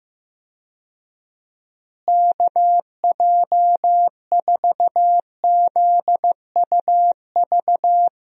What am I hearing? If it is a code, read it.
Morse code, heard as a keyed tone, about 15 wpm, 700 Hz. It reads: KJ4ZUV
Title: USA Call Signs - 6 Characters - Repeated 2x - 15wpm